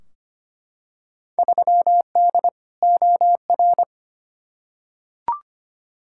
Morse code, heard as a keyed tone, about 25 wpm, 700 Hz. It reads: 3D OR
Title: Field Day Exchanges Encore - 25wpm